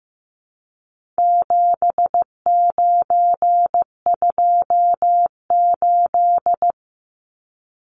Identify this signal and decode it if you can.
Morse code, heard as a keyed tone, about 15 wpm, 700 Hz. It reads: 7928
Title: Numbers - 4-Digit 15wpm